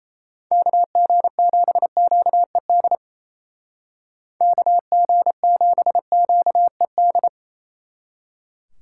Morse code, heard as a keyed tone, about 34 wpm, 700 Hz. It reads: KG7QED KG7QED
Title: Speed Racing - USA 2x3 Call Signs 28wpm